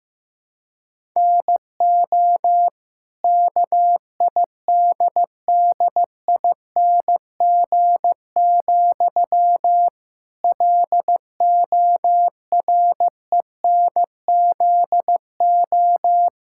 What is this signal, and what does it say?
Morse code, heard as a keyed tone, about 15 wpm, 700 Hz. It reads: NO KIDDING, LORENZO
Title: Pangram Challenge v2 - 15wpm